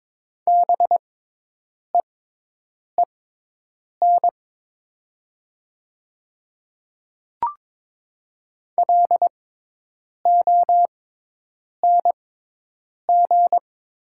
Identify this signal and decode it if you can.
Morse code, heard as a keyed tone, about 22 wpm, spaced as for 8 wpm, 700 Hz. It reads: BEEN LONG
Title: 6x Character Spacing - Top 200 Words - 22wpm